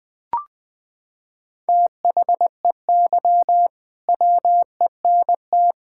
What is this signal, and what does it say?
Morse code, heard as a keyed tone, about 21 wpm, 700 Hz. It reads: THEY WENT
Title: Sentences from Top 1600 Words 20wpm